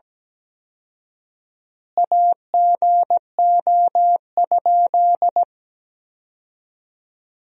Morse code, heard as a keyed tone, about 17 wpm, 700 Hz. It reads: AGO?
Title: 4x Word-Spacing - Sentences from Top 500 Words - 17wpm